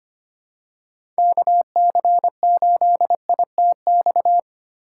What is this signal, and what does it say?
Morse code, heard as a keyed tone, about 25 wpm, 700 Hz. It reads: KC8ITX
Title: USA Call Signs - 6 Characters - 4x Character Spacing - 25wpm